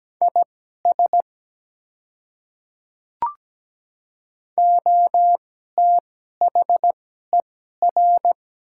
Morse code, heard as a keyed tone, about 17 wpm, spaced as for 12 wpm, 700 Hz. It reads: IS OTHER
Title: 2x Character Spacing - Top 100 Words - 17wpm